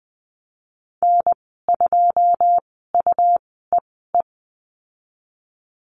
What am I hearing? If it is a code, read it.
Morse code, heard as a keyed tone, about 20 wpm, spaced as for 15 wpm, 700 Hz. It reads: N2UEE
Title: 128 Lesson 21g - Number 6 - Call Signs 20wpm